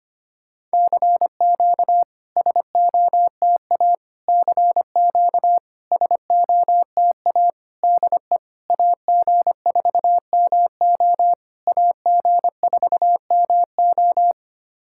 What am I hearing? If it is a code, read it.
Morse code, heard as a keyed tone, about 25 wpm, 700 Hz. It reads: CQ SOTA CQ SOTA DE AG4MO AG4MO
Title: SOTA QSO Practice 25wpm